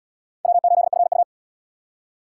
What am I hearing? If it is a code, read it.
Morse code, heard as a keyed tone, about 66 wpm, 690 Hz. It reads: W8FU